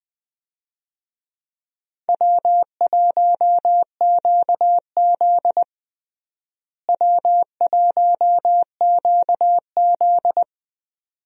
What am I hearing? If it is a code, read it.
Morse code, heard as a keyed tone, about 20 wpm, 700 Hz. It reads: W1QZ W1QZ
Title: USA Call Signs - 4 Characters - Repeated 2x - 20wpm